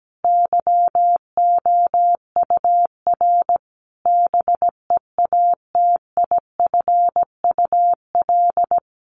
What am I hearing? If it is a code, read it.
Morse code, heard as a keyed tone, about 17 wpm, 700 Hz. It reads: YOUR BEATIFUL
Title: Sentences from Top 1300 Words 17wpm